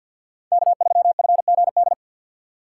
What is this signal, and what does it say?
Morse code, heard as a keyed tone, about 52 wpm, 700 Hz. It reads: K3FGD
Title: USA Call Signs - 3x Character Spacing - 50wpm